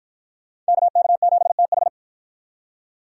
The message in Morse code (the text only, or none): KK7TH